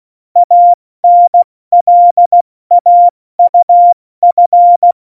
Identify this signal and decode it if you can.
Morse code, heard as a keyed tone, about 16 wpm, spaced as for 14 wpm, 700 Hz. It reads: ANLAUF
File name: Morsekurs / Witze / Witz 16. April 2019, 70 F80